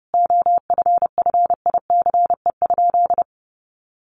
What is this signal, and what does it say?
Morse code, heard as a keyed tone, about 30 wpm, 700 Hz. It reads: OFFICE?
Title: Sentences from Top 1000 Words 30wpm